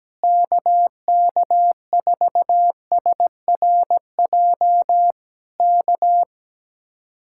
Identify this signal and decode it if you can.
Morse code, heard as a keyed tone, about 17 wpm, 700 Hz. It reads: KK4SRJ K
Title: Speed-Racing Common QSO Phrases - 17wpm